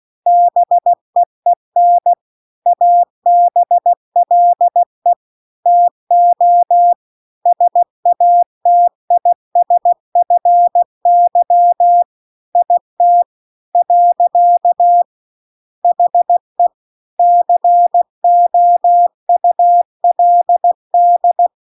Morse code, heard as a keyed tone, about 16 wpm, 700 Hz. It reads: BEEN ABLE TO SATISFY IT . HE COULD